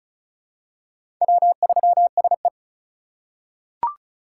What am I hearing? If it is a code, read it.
Morse code, heard as a keyed tone, about 36 wpm, 700 Hz. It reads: W3SE
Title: USA Call Signs - 4 Characters - 2x Character Spacing - 35wpm